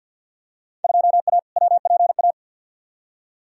Morse code, heard as a keyed tone, about 47 wpm, 700 Hz. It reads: 2A WPA
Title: Field Day Exchanges - 50wpm